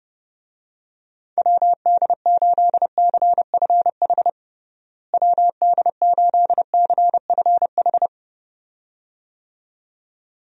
Morse code, heard as a keyed tone, about 32 wpm, 700 Hz. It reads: WD8CFH WD8CFH
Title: USA Call Signs - 6 Characters - Repeated 2x - 30wpm